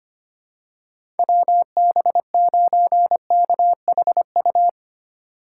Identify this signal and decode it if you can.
Morse code, heard as a keyed tone, about 25 wpm, 700 Hz. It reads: WB9KHU
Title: USA Call Signs - 3x Character Spacing - 25wpm